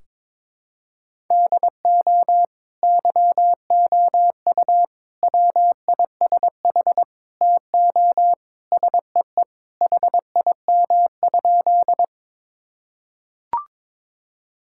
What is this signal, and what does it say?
Morse code, heard as a keyed tone, about 22 wpm, 700 Hz. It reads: DO YOU WISH TO SEE HIM?
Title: Sentences from Top 1600 Words 22wpm